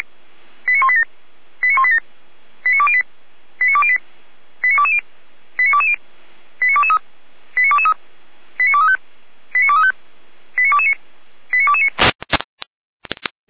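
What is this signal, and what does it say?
Five-tone selective call (ZVEI), heard as quick runs of five tones, tones of about 70 ms. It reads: 89187 89187 89208 89208 89220 89220 89223 89223 89235 89235 89229 89229
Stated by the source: ZVEI five-tone call received by SDR, from the feuerwehr-orlamuende.org fire brigade audio archive.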